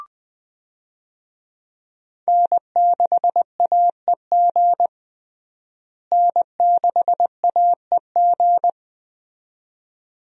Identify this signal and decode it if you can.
Morse code, heard as a keyed tone, about 20 wpm, 700 Hz. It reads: N6AEG N6AEG
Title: USA Call Signs - 5 Characters - Repeated 2x - 20wpm